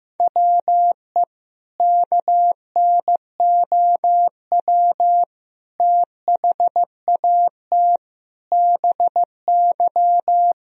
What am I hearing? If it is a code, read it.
Morse code, heard as a keyed tone, about 15 wpm, 700 Hz. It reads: WE KNOW THAT BY
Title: Sentences from Top 1000 Words - Review Part 3 - 15wpm